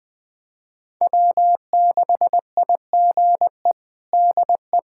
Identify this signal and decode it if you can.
Morse code, heard as a keyed tone, about 21 wpm, 700 Hz. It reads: W6IGE DE
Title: USA Callsign DE Callsign - 5 Characters - 20wpm